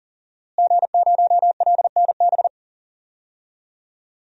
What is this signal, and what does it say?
Morse code, heard as a keyed tone, about 40 wpm, 700 Hz. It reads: G0LNB
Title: DXCC-100 Call Signs - 5 Characters - Repeated 3x - 40wpm